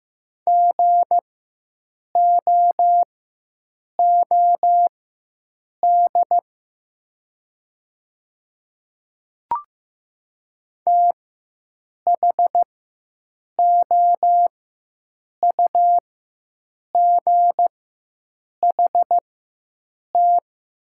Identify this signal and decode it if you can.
Morse code, heard as a keyed tone, about 15 wpm, spaced as for 7 wpm, 700 Hz. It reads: GOOD THOUGHT
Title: 4x Character Spacing - Top 500 Words - 15wpm